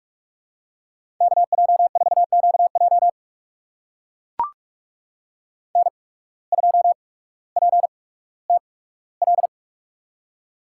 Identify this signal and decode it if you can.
Morse code, heard as a keyed tone, about 44 wpm, spaced as for 19 wpm, 700 Hz. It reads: KJ4QJ N2PTL
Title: USA Call Signs - 5 Characters - 8x Character Spacing - 45wpm